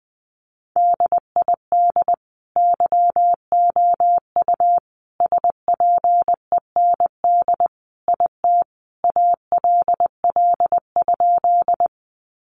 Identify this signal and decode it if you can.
Morse code, heard as a keyed tone, about 20 wpm, 700 Hz. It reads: DID YOU SPEND IT ALL?
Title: Sentences from Top 800 Words 20wpm